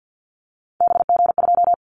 Morse code, heard as a keyed tone, about 48 wpm, 700 Hz. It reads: BG2
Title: Callsign Prefixes for Top 100 DXCC - 50wpm